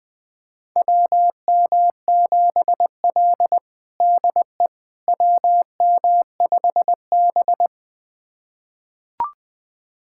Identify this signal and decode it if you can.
Morse code, heard as a keyed tone, about 21 wpm, 700 Hz. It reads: WM7L DE WM5B
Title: USA Callsign DE Callsign - 4 Characters - 20wpm